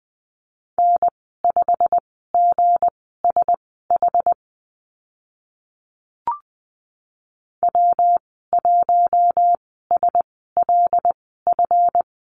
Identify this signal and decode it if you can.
Morse code, heard as a keyed tone, about 20 wpm, spaced as for 14 wpm, 700 Hz. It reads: N5GSH W1SLF
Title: 161 Lesson 26g - Letter g - Call Signs 20wpm